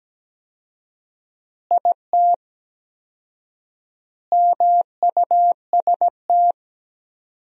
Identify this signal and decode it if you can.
Morse code, heard as a keyed tone, about 17 wpm, 700 Hz. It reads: IT MUST
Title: 4x Word-Spacing - Sentences from Top 500 Words - 17wpm